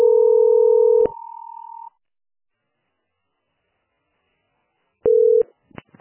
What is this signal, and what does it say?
Ringback tone (North America), 440 + 480 Hz.